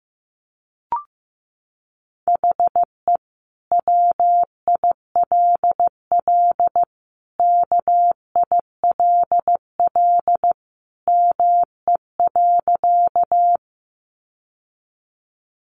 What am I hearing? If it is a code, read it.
Morse code, heard as a keyed tone, about 15 wpm, 700 Hz. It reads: HE WILL KILL ME.
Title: Sentences from Top 1400 Words 15wpm